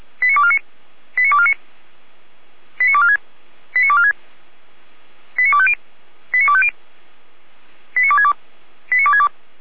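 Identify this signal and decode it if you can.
Five-tone selective call (ZVEI), heard as quick runs of five tones, tones of about 70 ms. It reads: 89249 89249 89246 89246 89250 89250 89262 89262
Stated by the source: ZVEI five-tone selective call received on an SDR.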